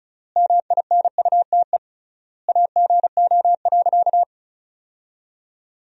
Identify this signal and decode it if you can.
Morse code, heard as a keyed tone, about 34 wpm, 700 Hz. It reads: MINUTE AGO.
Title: 3x Word-Spacing - Sentences from Top 500 Words - 35wpm